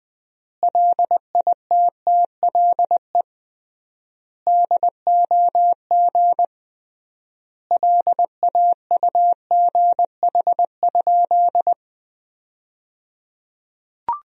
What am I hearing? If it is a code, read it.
Morse code, heard as a keyed tone, about 20 wpm, 700 Hz. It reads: LITTLE DOG LAUGH?
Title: 3x Word-Spacing - Sentences from Top 500 Words - 20wpm